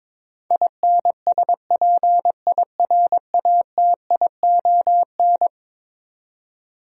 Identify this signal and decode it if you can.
Morse code, heard as a keyed tone, about 22 wpm, 700 Hz. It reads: INSPIRATION
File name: practice / 22wpm / Eleven Letter Words - 22wpm